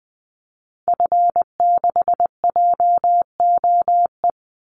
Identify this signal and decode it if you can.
Morse code, heard as a keyed tone, about 20 wpm, 700 Hz. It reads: F6JOE
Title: Call Signs - 20wpm